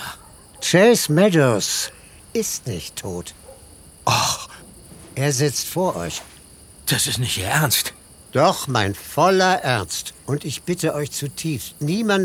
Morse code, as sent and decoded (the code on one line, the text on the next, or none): none